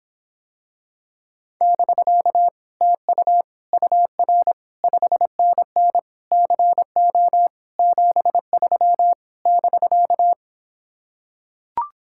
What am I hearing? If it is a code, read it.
Morse code, heard as a keyed tone, about 26 wpm, 700 Hz. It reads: <BK> TU UR 5NN CO 73 <BK>